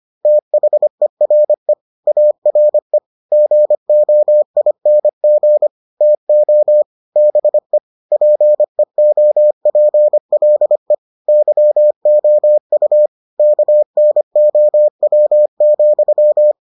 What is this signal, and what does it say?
Morse code, heard as a keyed tone, about 25 wpm, 600 Hz. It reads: THERE ARE GOING TO BE PEOPLE YOU KNOW,